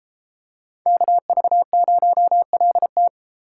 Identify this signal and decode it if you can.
Morse code, heard as a keyed tone, about 32 wpm, 700 Hz. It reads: KV0LT